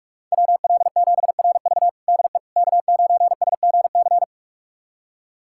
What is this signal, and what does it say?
Morse code, heard as a keyed tone, about 46 wpm, 700 Hz. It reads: WL7RV DE K9SGC
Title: USA Callsign DE Callsign - 5 Characters - 45wpm